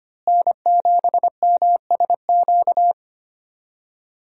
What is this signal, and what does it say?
Morse code, heard as a keyed tone, about 25 wpm, 700 Hz. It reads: N7MSQ
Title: USA Call Signs - 5 Characters - Repeated 3x - 25wpm